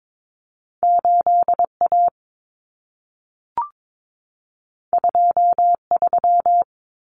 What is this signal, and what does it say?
Morse code, heard as a keyed tone, about 22 wpm, 700 Hz. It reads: 8A 23
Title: Common Cut Numbers - 2-Digit 22wpm